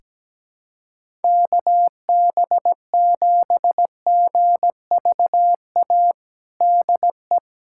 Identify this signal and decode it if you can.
Morse code, heard as a keyed tone, about 17 wpm, 700 Hz. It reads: KB7GVA DE
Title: USA Callsign DE Callsign - 6 Characters - 17wpm